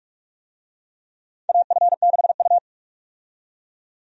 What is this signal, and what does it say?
Morse code, heard as a keyed tone, about 46 wpm, 700 Hz. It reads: AF6U
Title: USA Call Signs - 4 Characters - 45wpm